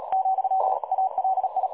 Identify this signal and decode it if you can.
Morse code, heard as a keyed tone, about 38 wpm, 760 Hz. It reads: , ALI
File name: dlazaawansowanych / nagraniazpasma / websdr recording 3565.0kHz